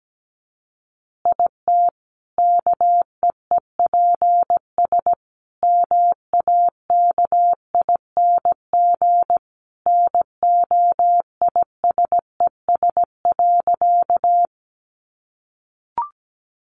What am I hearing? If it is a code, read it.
Morse code, heard as a keyed tone, about 17 wpm, 700 Hz. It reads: IT KEEPS MAKING NOISES.
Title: Additional Easy Phrases 17wpm